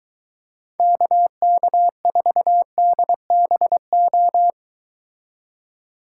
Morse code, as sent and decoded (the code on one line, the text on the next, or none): -.- -.- ....- -.. -... ---
KK4DBO